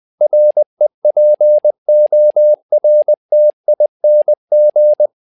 Morse code, heard as a keyed tone, about 20 wpm, 600 Hz. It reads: REPORTING
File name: tower trouble 20wpm